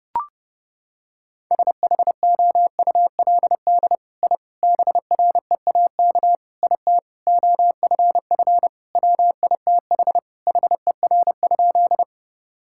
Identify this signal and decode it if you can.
Morse code, heard as a keyed tone, about 30 wpm, 700 Hz. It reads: SHOULD I BREAK IT OFF WITH HER?
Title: Sentences from Top 1000 Words - Review Part 6 - 30wpm